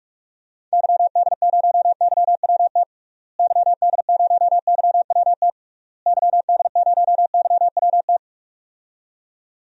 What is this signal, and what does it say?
Morse code, heard as a keyed tone, about 46 wpm, 700 Hz. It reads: YD0YWT YD0YWT YD0YWT
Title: DXCC-100 Call Signs - 6 Characters - Repeated 3x - 45wpm